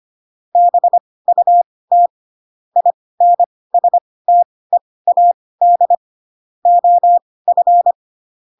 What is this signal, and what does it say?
Morse code, heard as a keyed tone, about 25 wpm, spaced as for 18 wpm, 700 Hz. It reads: BUT INSTEAD OF
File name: Pinocchio ch3 12 25 ik1hge